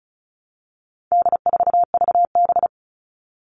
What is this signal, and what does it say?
Morse code, heard as a keyed tone, about 34 wpm, 700 Hz. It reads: D4VB
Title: All Cut Numbers - 4-Digit 35wpm